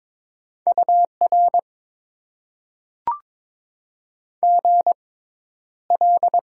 Morse code, heard as a keyed tone, about 22 wpm, spaced as for 11 wpm, 700 Hz. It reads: UR GL
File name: Top 90 QSO Elements - 6x Character Spacing - 22wpm